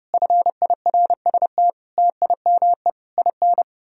Morse code, heard as a keyed tone, about 35 wpm, 700 Hz. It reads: FIRST TIME IN
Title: Speed-Racing - Sets of 5 Words 30wpm